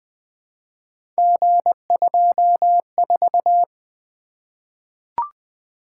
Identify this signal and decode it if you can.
Morse code, heard as a keyed tone, about 21 wpm, 700 Hz. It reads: G24
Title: Three Letter-Number Combinations - 20wpm